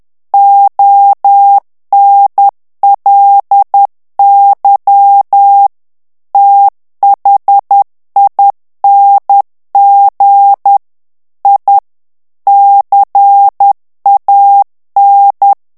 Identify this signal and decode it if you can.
Morse code, heard as a keyed tone, about 10 wpm, 800 Hz. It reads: ONLY THING I CAN
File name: morse mp3 / ANGLAIS texte clair 2 vitesse 10 mots minute